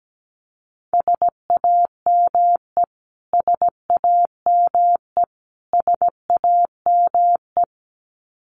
Morse code, heard as a keyed tone, about 17 wpm, 700 Hz. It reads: SAME SAME SAME
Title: Top 200 Words Repeated 17wpm